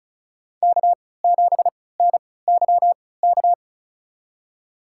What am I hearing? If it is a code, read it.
Morse code, heard as a keyed tone, about 36 wpm, spaced as for 20 wpm, 700 Hz. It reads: K7NYK
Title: USA Call Signs - 3x Character Spacing - 35wpm